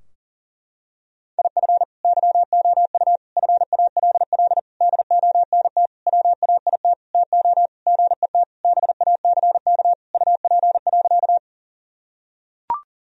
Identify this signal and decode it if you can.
Morse code, heard as a keyed tone, about 40 wpm, 700 Hz. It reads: IF YOU FALL DONT WAIT TO GET BACK UP.
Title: Sentences from Top 1000 Words - Review Part 2 - 40wpm